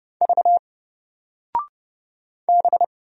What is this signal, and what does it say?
Morse code, heard as a keyed tone, about 30 wpm, 700 Hz. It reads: V B
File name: Single-Letter-Number-30wpm